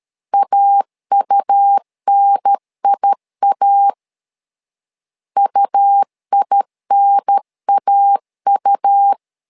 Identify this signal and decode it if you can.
Morse code, heard as a keyed tone, about 13 wpm, 800 Hz. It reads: AUNIA UINAU
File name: LEZIONI / 2 UHDN / 2 UHDN 12wpm 1456